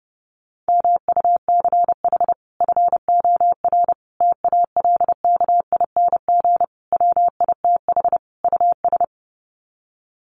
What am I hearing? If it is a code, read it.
Morse code, heard as a keyed tone, about 30 wpm, 700 Hz. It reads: MUCH FOR TALKING WITH US
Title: Sets of 5 Words Encore 30wpm